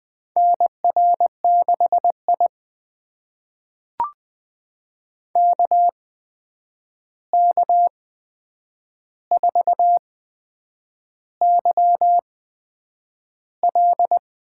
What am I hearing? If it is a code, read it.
Morse code, heard as a keyed tone, about 20 wpm, spaced as for 8 wpm, 700 Hz. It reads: NR6I KK4YL
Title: USA Call Signs - 8x Character Spacing - 20wpm